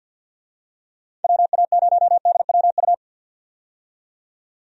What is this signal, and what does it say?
Morse code, heard as a keyed tone, about 52 wpm, 700 Hz. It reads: WA0DWU
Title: USA Call Signs - 6 Characters - 50wpm